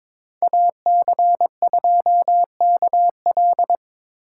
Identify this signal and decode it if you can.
Morse code, heard as a keyed tone, about 22 wpm, 700 Hz. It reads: AC2KL
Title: USA Call Signs - 5 Characters - 3x Character Spacing - 22wpm